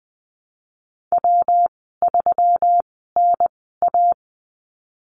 Morse code, heard as a keyed tone, about 20 wpm, spaced as for 15 wpm, 700 Hz. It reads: W3NA
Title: 140 Lesson 23g - Letter f - Call Signs 20wpm